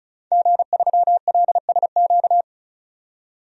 Morse code, heard as a keyed tone, about 34 wpm, 700 Hz. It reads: G3LSQ
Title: DXCC-100 Call Signs - 5 Characters - 35wpm